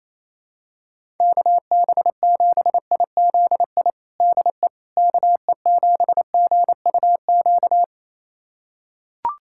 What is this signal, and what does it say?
Morse code, heard as a keyed tone, about 28 wpm, 700 Hz. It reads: KB7IZI DE KE7GUQ
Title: USA Callsign DE Callsign - 6 Characters - 28wpm